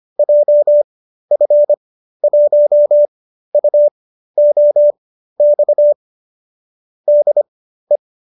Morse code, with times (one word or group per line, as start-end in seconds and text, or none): 0.2-5.9 JF1UOX
7.1-8.0 DE